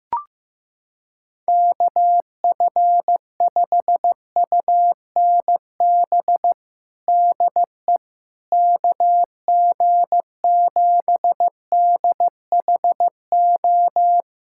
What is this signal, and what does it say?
Morse code, heard as a keyed tone, about 15 wpm, 700 Hz. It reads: KF5UNB DE KG7DHO
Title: USA Callsign DE Callsign - 6 Characters - 15wpm